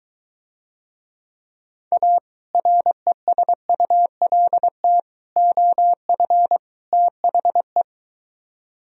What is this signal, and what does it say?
Morse code, heard as a keyed tone, about 23 wpm, 700 Hz. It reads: A RESULT OF THE